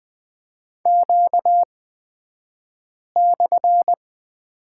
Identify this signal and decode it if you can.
Morse code, heard as a keyed tone, about 20 wpm, spaced as for 5 wpm, 700 Hz. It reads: Q/
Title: 235 Lesson 38d - Letter q - Two Characters 20wpm